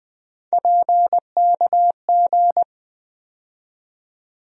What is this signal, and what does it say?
Morse code, heard as a keyed tone, about 20 wpm, 700 Hz. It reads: PKG
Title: USA Call Sign Postfixes - 3 Letters 20wpm